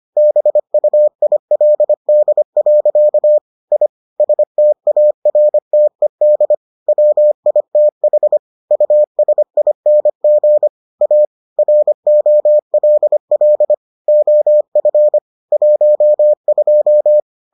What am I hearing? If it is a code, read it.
Morse code, heard as a keyed tone, about 25 wpm, 600 Hz. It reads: BUILD. I STARTED WITH USING A ROLL OF 12